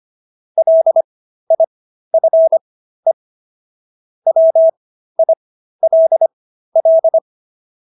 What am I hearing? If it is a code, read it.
Morse code, heard as a keyed tone, about 25 wpm, spaced as for 13 wpm, 650 Hz. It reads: LIFE WILL